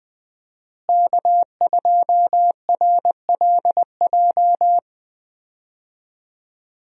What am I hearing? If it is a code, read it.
Morse code, heard as a keyed tone, about 20 wpm, 700 Hz. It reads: K2RLJ
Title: USA Call Signs - 5 Characters - Repeated 2x - 20wpm